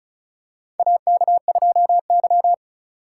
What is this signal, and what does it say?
Morse code, heard as a keyed tone, about 34 wpm, 700 Hz. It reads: AK2Y
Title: USA Call Signs - 4 Characters - 4x Character Spacing - 35wpm